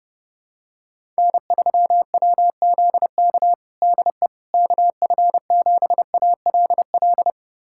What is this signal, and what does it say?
Morse code, heard as a keyed tone, about 30 wpm, 700 Hz. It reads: N3WZK DE KF7ALL
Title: USA Callsign DE Callsign 30wpm